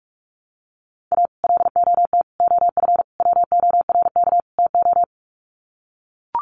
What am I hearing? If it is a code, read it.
Morse code, heard as a keyed tone, about 44 wpm, 700 Hz. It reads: A LOT OF WORK TO
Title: Sets of 5 Words Encore2 45wpm